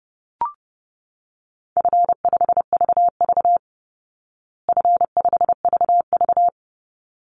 Morse code, heard as a keyed tone, about 30 wpm, 700 Hz. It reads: F5VV F5VV
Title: Easy Call Signs Repeated 2x - 30wpm